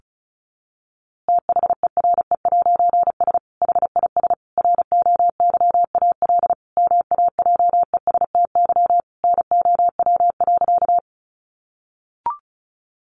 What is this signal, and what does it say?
Morse code, heard as a keyed tone, about 34 wpm, 700 Hz. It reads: THERE'S HIS ROYAL MAJESTY NOW.
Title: A Sound Of Thunder 35wpm